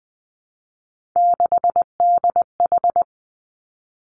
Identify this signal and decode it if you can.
Morse code, heard as a keyed tone, about 20 wpm, 700 Hz. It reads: 6DH
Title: Three Challenging Letter-Number Combinations 20wpm